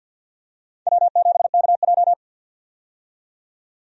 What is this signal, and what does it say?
Morse code, heard as a keyed tone, about 47 wpm, 700 Hz. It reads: W7KJ
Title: USA Call Signs - 4 Characters - 50wpm